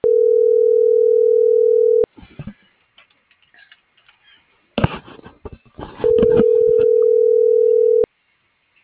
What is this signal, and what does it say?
Ringback tone (North America), 440 + 480 Hz.
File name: CENTRAL MABE / out antonio.garcia